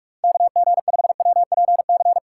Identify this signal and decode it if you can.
Morse code, heard as a keyed tone, about 47 wpm, 700 Hz. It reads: KG5WPC
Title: USA Call Signs - 2x Character Spacing - 45wpm